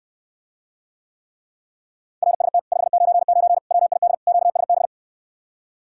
Morse code, heard as a keyed tone, about 68 wpm, 700 Hz. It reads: RST 599 WID QSB